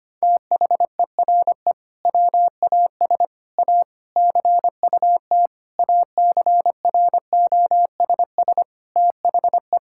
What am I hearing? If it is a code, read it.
Morse code, heard as a keyed tone, about 25 wpm, 700 Hz. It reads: THERE WAS A CUT ACROSS THE
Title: Sentences from Top 1000 Words - Review Part 5 - 25wpm